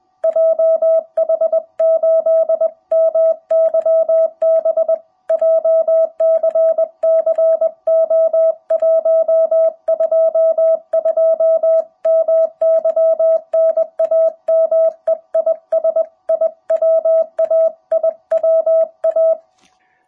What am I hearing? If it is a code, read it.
Morse code, heard as a keyed tone, about 20 wpm, 660 Hz. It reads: JH8MYB JCC O122 MY NAME IS IWA IWA
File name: jh8cbh / 2021cw